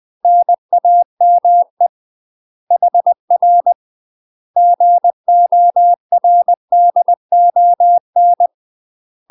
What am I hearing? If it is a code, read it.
Morse code, heard as a keyed tone, about 20 wpm, 700 Hz. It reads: NAME HR GORDON
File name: UK MASTER QSOs TRAINING / 20WPM / UK MASTER QSOs 0044